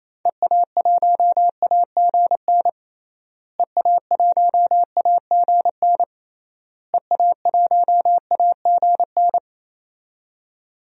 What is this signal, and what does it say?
Morse code, heard as a keyed tone, about 27 wpm, 700 Hz. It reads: EA1AGN EA1AGN EA1AGN
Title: DXCC-100 Call Signs - 6 Characters - Repeated 3x - 28wpm